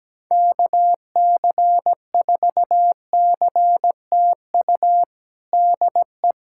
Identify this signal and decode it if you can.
Morse code, heard as a keyed tone, about 17 wpm, 700 Hz. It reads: KC4CTU DE
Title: USA Callsign DE Callsign - 6 Characters - 17wpm